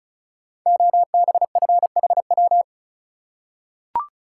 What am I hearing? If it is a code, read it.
Morse code, heard as a keyed tone, about 36 wpm, 700 Hz. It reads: OBFHW
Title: Code Groups - 5 Letters - 35wpm